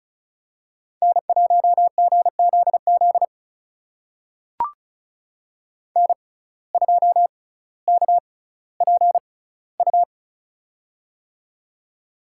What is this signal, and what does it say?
Morse code, heard as a keyed tone, about 36 wpm, spaced as for 18 wpm, 700 Hz. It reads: N1GZZ N2KPU
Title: USA Call Signs - 5 Characters - 6x Character Spacing - 35wpm